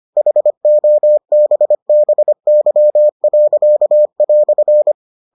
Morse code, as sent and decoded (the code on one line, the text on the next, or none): .... --- -... -... -.-- .-.-.- .-..-.
HOBBY."